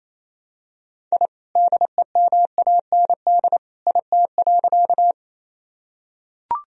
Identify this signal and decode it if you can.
Morse code, heard as a keyed tone, about 28 wpm, 700 Hz. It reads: I DEMAND IT.
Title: Sentences from Top 1000 Words - Review Part 4 - 28wpm